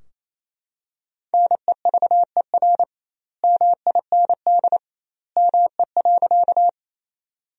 Morse code, heard as a keyed tone, about 28 wpm, 700 Hz. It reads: NEVER MIND ME.
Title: 2x Word-Spacing - Sentences from Top 500 Words - 28wpm